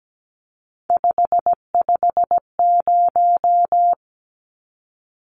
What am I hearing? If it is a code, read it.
Morse code, heard as a keyed tone, about 17 wpm, 700 Hz. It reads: 550
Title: Numbers - 3-Digit 17wpm